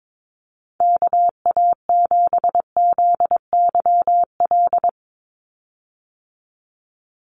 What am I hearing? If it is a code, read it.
Morse code, heard as a keyed tone, about 22 wpm, 700 Hz. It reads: KA7ZYL
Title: USA Call Signs 22wpm